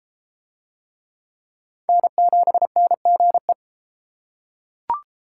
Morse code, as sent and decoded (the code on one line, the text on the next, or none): -. --... -. --. .
N7NGE